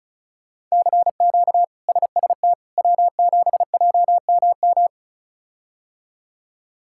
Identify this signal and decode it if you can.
Morse code, heard as a keyed tone, about 34 wpm, 700 Hz. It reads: CQ SST W7JMM
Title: K1USN SST QSO Practice - 35wpm